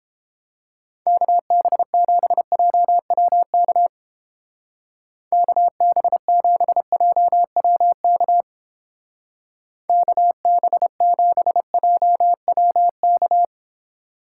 Morse code, heard as a keyed tone, about 29 wpm, 700 Hz. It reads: KB7JWK KB7JWK KB7JWK